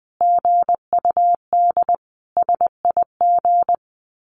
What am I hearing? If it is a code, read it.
Morse code, heard as a keyed tone, about 20 wpm, 700 Hz. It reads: GUD SIG
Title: Common QSO Phrases 20wpm